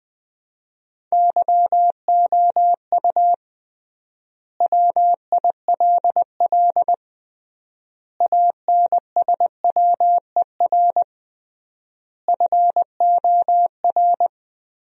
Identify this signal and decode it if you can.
Morse code, heard as a keyed tone, about 20 wpm, 700 Hz. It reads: YOU WILL ANSWER FOR
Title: 3x Word-Spacing - Sentences from Top 300 Words - 20wpm